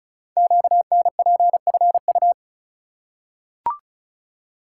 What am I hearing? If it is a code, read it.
Morse code, heard as a keyed tone, about 34 wpm, 700 Hz. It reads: QNPFU
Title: Code Groups - 5 Letters - 35wpm